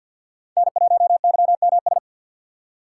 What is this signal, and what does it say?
Morse code, heard as a keyed tone, about 48 wpm, 700 Hz. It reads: N1YMS
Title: USA Call Signs - 5 Characters - 2x Character Spacing - 50wpm